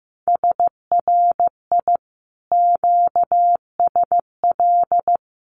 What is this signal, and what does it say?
Morse code, heard as a keyed tone, about 15 wpm, 700 Hz. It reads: SRI QSL
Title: Common QSO Phrases Encore 15wpm